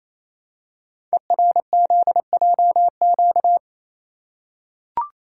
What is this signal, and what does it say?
Morse code, heard as a keyed tone, about 29 wpm, 700 Hz. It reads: ERZJQ
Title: Code Groups - 5 Letters - 28wpm